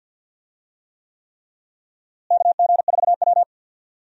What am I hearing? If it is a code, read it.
Morse code, heard as a keyed tone, about 53 wpm, 700 Hz. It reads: KG4W
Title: USA Call Signs - 4 Characters - Repeated 3x - 50wpm